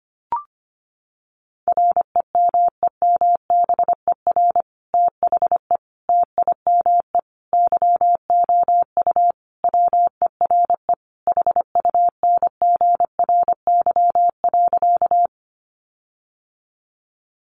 Morse code, heard as a keyed tone, about 25 wpm, 700 Hz. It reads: REMEMBER THE TIME YOU WERE HUNGRY.
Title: Sentences from Top 1400 Words 25wpm